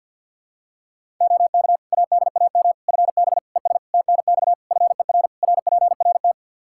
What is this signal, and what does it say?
Morse code, heard as a keyed tone, about 49 wpm, 700 Hz. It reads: OK ADAM FB ES TNX FER RPRT
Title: Practice QSO - Exchange 2 - Info - 50wpm